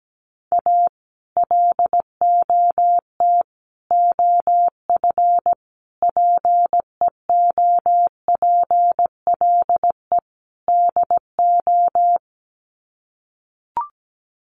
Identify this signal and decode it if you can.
Morse code, heard as a keyed tone, about 17 wpm, 700 Hz. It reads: A LOT OF PEOPLE DO
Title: Sets of 5 Words 17wpm